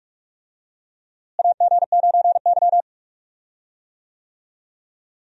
Morse code, heard as a keyed tone, about 46 wpm, 700 Hz. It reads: AG9Y
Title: USA Call Signs - 4 Characters - Repeated 2x - 45wpm